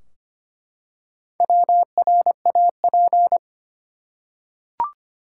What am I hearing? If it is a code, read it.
Morse code, heard as a keyed tone, about 25 wpm, 700 Hz. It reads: WRAP
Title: Four Letter Words - 25wpm